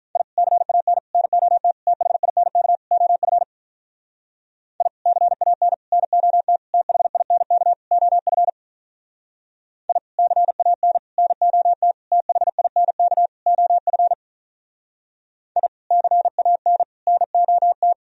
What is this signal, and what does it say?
Morse code, heard as a keyed tone, about 46 wpm, 700 Hz. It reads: I CAN NOT THINK OF I CAN NOT THINK OF I CAN NOT THINK OF I CAN NOT